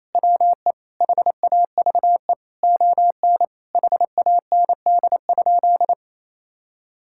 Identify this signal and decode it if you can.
Morse code, heard as a keyed tone, about 28 wpm, 700 Hz. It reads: WE HAVE ON HAND?
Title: Sentences from Top 1000 Words - Review Part 5 - 28wpm